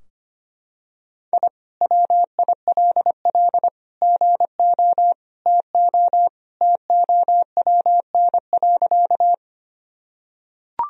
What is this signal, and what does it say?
Morse code, heard as a keyed tone, about 25 wpm, 700 Hz. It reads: I WILL GO TO TOWN.
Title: Sentences from Top 1000 Words - Review Part 2 - 25wpm